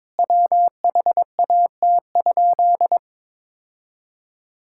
Morse code, heard as a keyed tone, about 22 wpm, 700 Hz. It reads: WHAT?
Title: Sentences from Top 1600 Words 22wpm